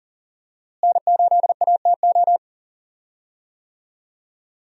Morse code, heard as a keyed tone, about 38 wpm, 700 Hz. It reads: N8ATO
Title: USA Call Signs - 5 Characters - Repeated 3x - 40wpm